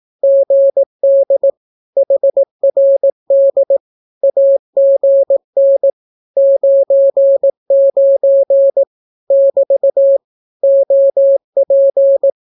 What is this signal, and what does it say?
Morse code, heard as a keyed tone, about 18 wpm, 550 Hz. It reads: GD HRD AGN 99 <BT> OP
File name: naukapodstaw / skroty / skroty